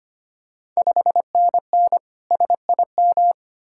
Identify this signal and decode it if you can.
Morse code, heard as a keyed tone, about 25 wpm, 700 Hz. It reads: 5NN SIM
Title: Ontario QSO Party Exchange 25wpm